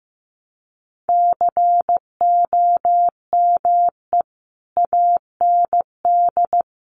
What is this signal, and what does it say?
Morse code, heard as a keyed tone, about 15 wpm, 700 Hz. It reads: COME AND
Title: Sentences from Top 200 Words 15wpm